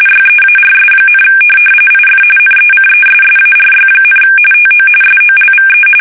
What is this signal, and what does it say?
RTTY, about 45.45 baud. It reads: YEARS, IF NECESSARY, TO SOLVE
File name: fsk850 / benjamin-g-lamme